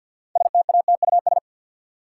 Morse code, heard as a keyed tone, about 52 wpm, 700 Hz. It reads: STATUS
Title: Sentences from Top 1400 Words - New Word Drill Alt - 50wpm